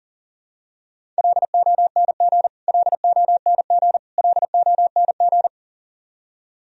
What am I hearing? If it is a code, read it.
Morse code, heard as a keyed tone, about 40 wpm, 700 Hz. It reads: LONG LONG LONG
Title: Top 100 Words Repeated 40wpm